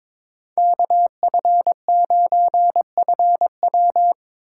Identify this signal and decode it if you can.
Morse code, heard as a keyed tone, about 22 wpm, 700 Hz. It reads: KF9FW
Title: USA Call Signs - 5 Characters - 3x Character Spacing - 22wpm